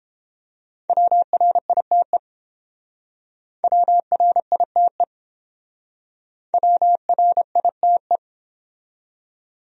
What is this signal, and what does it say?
Morse code, heard as a keyed tone, about 30 wpm, 700 Hz. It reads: WRITE WRITE WRITE